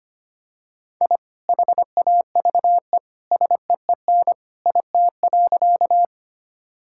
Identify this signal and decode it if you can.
Morse code, heard as a keyed tone, about 25 wpm, 700 Hz. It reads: I HAVE SEEN IT.
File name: practice / 25wpm / Sentences with 2-Word Phrases - 25wpm